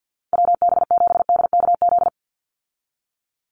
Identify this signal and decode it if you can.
Morse code, heard as a keyed tone, about 50 wpm, 700 Hz. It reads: WB7DKZ
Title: Call Signs Repeated 3x - 50wpm